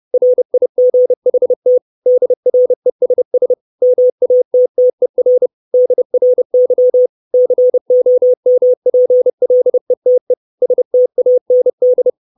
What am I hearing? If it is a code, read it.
Morse code, heard as a keyed tone, about 30 wpm, 500 Hz. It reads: RIGHT DRESS MATTER DRY COMPLETE STAND
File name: 632 Words 30WPM